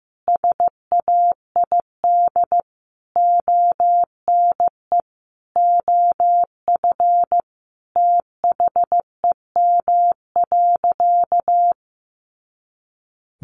Morse code, heard as a keyed tone, about 15 wpm, 700 Hz. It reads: SAID ONE OF THEM.
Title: Sentences from Top 300 Words 15wpm